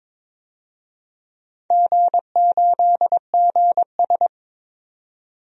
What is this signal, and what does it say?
Morse code, heard as a keyed tone, about 22 wpm, 700 Hz. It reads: G8GS
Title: DXCC-100 Call Signs - 4 Characters - 22wpm